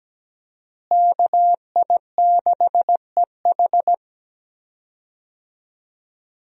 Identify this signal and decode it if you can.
Morse code, heard as a keyed tone, about 17 wpm, 700 Hz. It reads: KI6EH
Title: USA Call Signs - 5 Characters - Repeated 3x - 17wpm